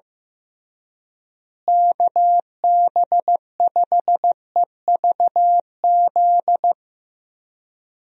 Morse code, heard as a keyed tone, about 15 wpm, 700 Hz. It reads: KB5EVZ
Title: USA Call Signs - 6 Characters - Repeated 2x - 15wpm